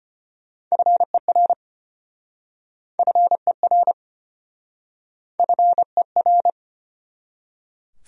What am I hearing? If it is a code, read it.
Morse code, heard as a keyed tone, about 34 wpm, 700 Hz. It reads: FER FER FER